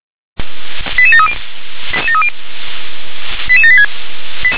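Five-tone selective call (ZVEI), heard as quick runs of five tones, tones of about 70 ms. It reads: 88730 88786
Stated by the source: ZVEI five-tone selective call received on SDR.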